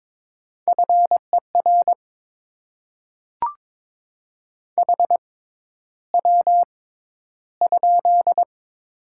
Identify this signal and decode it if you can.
Morse code, heard as a keyed tone, about 22 wpm, spaced as for 11 wpm, 700 Hz. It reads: FER HW?
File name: Top 20 QSO Elements - 6x Character Spacing - 22wpm